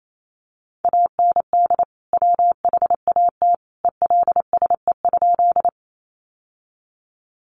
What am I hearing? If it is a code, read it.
Morse code, heard as a keyed tone, about 28 wpm, 700 Hz. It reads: AND WHAT ELSE?
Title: Sentences from Top 800 Words 28wpm